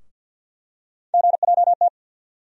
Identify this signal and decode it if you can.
Morse code, heard as a keyed tone, about 52 wpm, 700 Hz. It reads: GJT
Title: USA Call Sign Postfixes - 3 Letters 50wpm